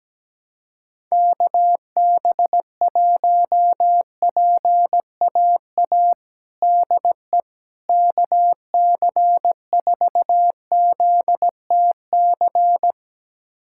Morse code, heard as a keyed tone, about 17 wpm, 700 Hz. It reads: KB1PAA DE KC4ZTC
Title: USA Callsign DE Callsign - 6 Characters - 17wpm